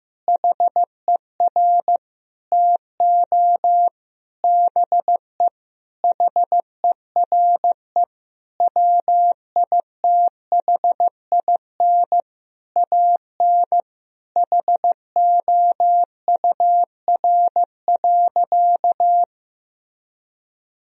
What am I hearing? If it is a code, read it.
Morse code, heard as a keyed tone, about 15 wpm, 700 Hz. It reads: HER TO BE HERE WITHIN AN HOUR.
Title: Sentences from Top 1000 Words - Review Part 4 - 15wpm